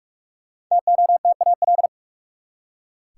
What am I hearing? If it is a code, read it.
Morse code, heard as a keyed tone, about 46 wpm, 700 Hz. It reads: TOTAL
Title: Five Letter Words - 45wpm